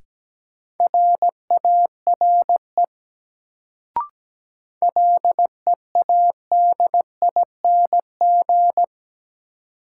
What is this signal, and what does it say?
Morse code, heard as a keyed tone, about 17 wpm, 700 Hz. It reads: RARE LEADING
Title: Sentences from Top 1100 Words - New Word Drill - 17wpm